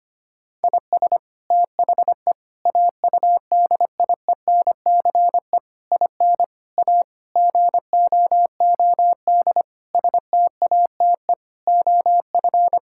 Morse code, heard as a keyed tone, about 25 wpm, 700 Hz. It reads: IS THE AUDIENCE IN A GOOD STATE OF
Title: Sentences from Top 1000 Words - Review Part 5 - 25wpm